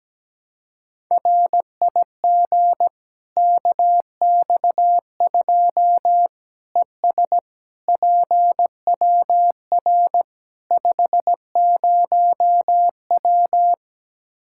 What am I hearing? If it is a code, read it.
Morse code, heard as a keyed tone, about 17 wpm, 700 Hz. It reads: RIG KX2 ES PWR 50W
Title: Practice QSO - Exchange 2 - Info - 17wpm